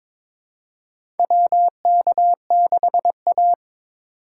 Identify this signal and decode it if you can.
Morse code, heard as a keyed tone, about 22 wpm, 700 Hz. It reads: WK6A
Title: USA Call Signs - 4 Characters - 4x Character Spacing - 22wpm